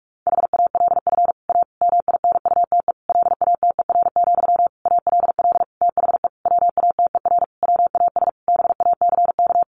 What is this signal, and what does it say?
Morse code, heard as a keyed tone, about 46 wpm, 700 Hz. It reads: HALF A MINUTE LATER, ALL THE WATER WAS BACK
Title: Sentences from Top 1100 Words 45wpm